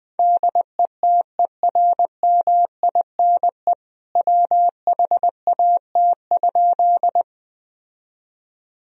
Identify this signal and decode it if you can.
Morse code, heard as a keyed tone, about 19 wpm, 700 Hz. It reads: DETERMINE WHAT?
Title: Sentences from Top 1000 Words - Review Part 4 - 20wpm